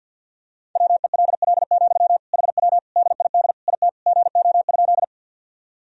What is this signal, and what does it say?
Morse code, heard as a keyed tone, about 48 wpm, 700 Hz. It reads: WELL, HW DID IT GO?
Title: Sentences from Top 200 Words and Abbreviations 50wpm